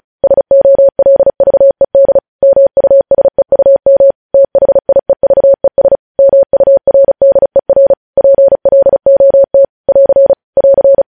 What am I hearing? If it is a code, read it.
Morse code, heard as a keyed tone, about 36 wpm, 550 Hz. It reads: SOLVED MUSEUM THIEVES MURDER PLOT <AR> <AR>
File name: podcast / News headlines-1x-35x35